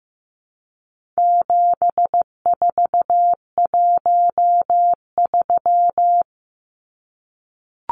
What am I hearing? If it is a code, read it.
Morse code, heard as a keyed tone, about 15 wpm, 700 Hz. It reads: 7413
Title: Numbers - 4-Digit 15wpm